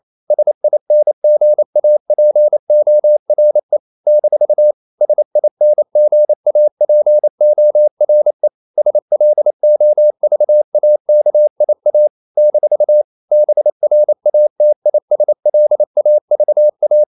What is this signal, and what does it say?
Morse code, heard as a keyed tone, about 28 wpm, 600 Hz. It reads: SINGAPORE - SINGAPORE SLOVAKIA - BRATISLAVA
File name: CW / Practice / NationalCapitals28